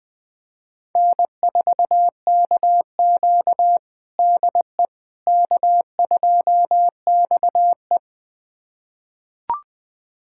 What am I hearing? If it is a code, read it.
Morse code, heard as a keyed tone, about 21 wpm, 700 Hz. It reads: N4KQ DE K2XE
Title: USA Callsign DE Callsign - 4 Characters - 20wpm